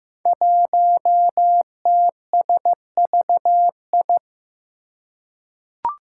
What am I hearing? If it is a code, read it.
Morse code, heard as a keyed tone, about 15 wpm, 700 Hz. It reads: 1TSVI
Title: Five Challenging Letter-Number Combinations 15wpm